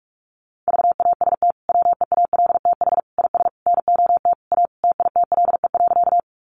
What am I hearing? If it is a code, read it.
Morse code, heard as a keyed tone, about 44 wpm, 700 Hz. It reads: VAST WEALTH IS NOT A TITLE.
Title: Sentences from Top 1300 Words 45wpm